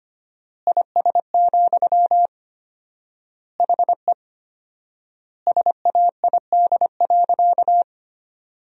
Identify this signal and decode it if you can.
Morse code, heard as a keyed tone, about 25 wpm, 700 Hz. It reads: IS, HE SAID.
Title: 4x Word-Spacing - Sentences from Top 200 Words - 25wpm